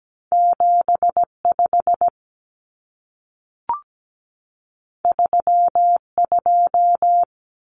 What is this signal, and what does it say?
Morse code, heard as a keyed tone, about 17 wpm, 700 Hz. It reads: 75 32
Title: Numbers - 2-Digit 17wpm